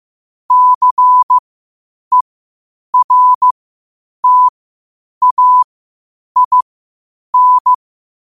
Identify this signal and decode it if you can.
Morse code, heard as a keyed tone, about 15 wpm, spaced as for 8 wpm, 1000 Hz. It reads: CERTAIN
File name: qotdfiles / 8wpm